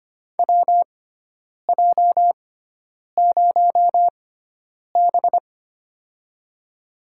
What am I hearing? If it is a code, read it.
Morse code, heard as a keyed tone, about 25 wpm, spaced as for 9 wpm, 700 Hz. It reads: WJ0B
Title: USA Call Signs - 6x Character Spacing - 25wpm